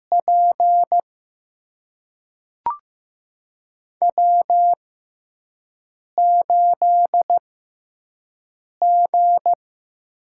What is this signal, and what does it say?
Morse code, heard as a keyed tone, about 15 wpm, spaced as for 5 wpm, 700 Hz. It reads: P W8G
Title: USA Call Signs - 5 Characters - 6x Character Spacing - 15wpm